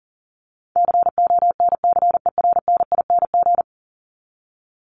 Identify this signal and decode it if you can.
Morse code, heard as a keyed tone, about 40 wpm, 700 Hz. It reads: CONCERNING
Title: Top 1751-2000 Words 40wpm